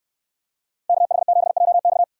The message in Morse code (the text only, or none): KH7JX